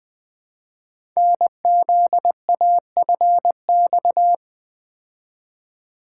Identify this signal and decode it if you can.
Morse code, heard as a keyed tone, about 20 wpm, 700 Hz. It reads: NZAFX
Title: Code Groups - 5 Letters - 20wpm